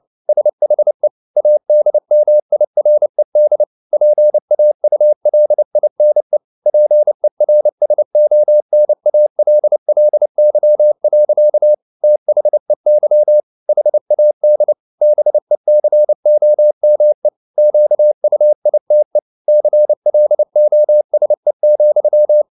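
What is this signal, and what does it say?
Morse code, heard as a keyed tone, about 29 wpm, 600 Hz. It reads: SHE ADMIRED PAULINE PERSONALLY. THEY HAD BECOME QUITE CLOSE,
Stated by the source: Morse code at 29 wpm, sending Ken Follett's novel Never in English, from CW med Gnister.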